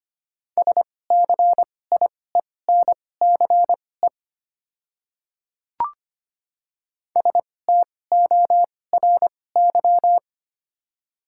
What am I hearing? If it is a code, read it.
Morse code, heard as a keyed tone, about 25 wpm, spaced as for 18 wpm, 700 Hz. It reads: SCIENCE STORY
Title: 2x Character Spacing - Top 500 Words - 25wpm